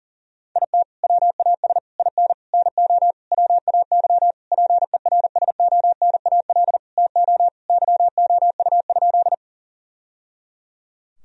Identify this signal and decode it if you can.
Morse code, heard as a keyed tone, about 37 wpm, 700 Hz. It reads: IT WAS IN NO WAY PERSONAL TO YOU?
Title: Sentences from Top 1000 Words - Review Part 4 - 40wpm